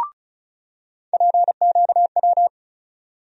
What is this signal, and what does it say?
Morse code, heard as a keyed tone, about 34 wpm, 700 Hz. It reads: PQW
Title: USA Call Sign Postfixes - 3 Letters 35wpm